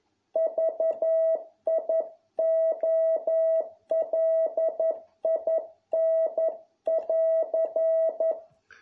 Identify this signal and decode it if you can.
Morse code, heard as a keyed tone, about 11 wpm, 650 Hz. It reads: VIOLIN<AR>